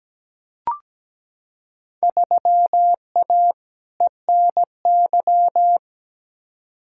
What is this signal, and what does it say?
Morse code, heard as a keyed tone, about 17 wpm, 700 Hz. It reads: 3A ENY
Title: Field Day Exchanges - 17wpm